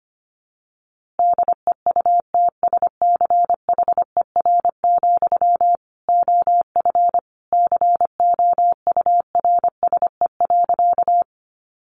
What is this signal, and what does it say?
Morse code, heard as a keyed tone, about 25 wpm, 700 Hz. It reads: DEUTSCHER, OF COURSE.
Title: A Sound Of Thunder 25wpm